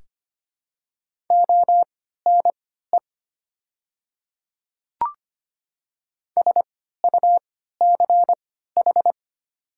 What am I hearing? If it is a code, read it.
Morse code, heard as a keyed tone, about 25 wpm, spaced as for 14 wpm, 700 Hz. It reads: ONE SUCH
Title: 3x Character Spacing - Top 200 Words - 25wpm